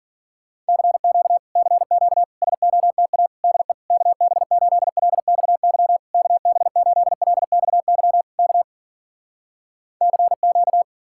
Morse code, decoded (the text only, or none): CQ CQ SOTA DE KB8LXY KB8LXY K CQ